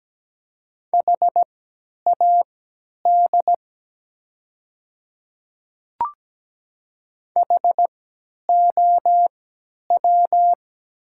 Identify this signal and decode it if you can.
Morse code, heard as a keyed tone, about 17 wpm, spaced as for 10 wpm, 700 Hz. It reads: HAD HOW
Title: 3x Character Spacing - Top 100 Words - 17wpm